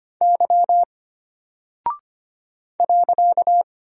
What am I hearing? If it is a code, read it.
Morse code, heard as a keyed tone, about 25 wpm, 700 Hz. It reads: Y .
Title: Single Character (Letter:Number:Punctuation:Prosign) Alt 25wpm